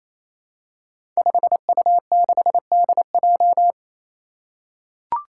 Morse code, heard as a keyed tone, about 28 wpm, 700 Hz. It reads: 5U6DJ
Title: Five Challenging Letter-Number Combinations 28wpm